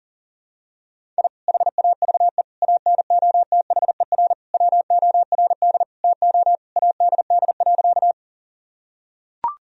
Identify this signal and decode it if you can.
Morse code, heard as a keyed tone, about 37 wpm, 700 Hz. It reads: I HAVE ANOTHER WORD TO ADD.
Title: Sentences from Top 1000 Words - Review Part 1 - 40wpm